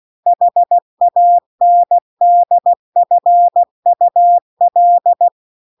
Morse code, heard as a keyed tone, about 16 wpm, 700 Hz. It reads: HANDFUL